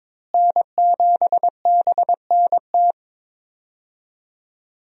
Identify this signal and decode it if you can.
Morse code, heard as a keyed tone, about 22 wpm, 700 Hz. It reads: N7BNT
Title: USA Call Signs - 5 Characters - Repeated 2x - 22wpm